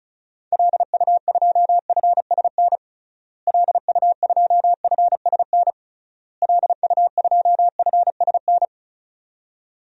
Morse code, heard as a keyed tone, about 36 wpm, 700 Hz. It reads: LU2FSN LU2FSN LU2FSN
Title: DXCC-100 Call Signs - 6 Characters - Repeated 3x - 35wpm